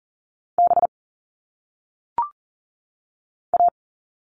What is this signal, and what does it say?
Morse code, heard as a keyed tone, about 40 wpm, 700 Hz. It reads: B A
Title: All Cut Numbers - 1-Digit 40wpm